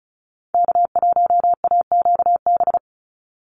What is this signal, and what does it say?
Morse code, heard as a keyed tone, about 34 wpm, 700 Hz. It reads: K1AQB
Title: Call Signs Repeated 2x - 35wpm